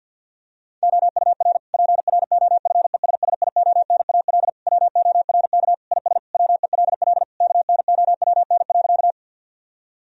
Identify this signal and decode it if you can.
Morse code, heard as a keyed tone, about 51 wpm, 700 Hz. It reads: OUR PROFESSIONAL WORK IS WELL KNOWN.
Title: Sentences from Top 1000 Words - Review Part 4 - 50wpm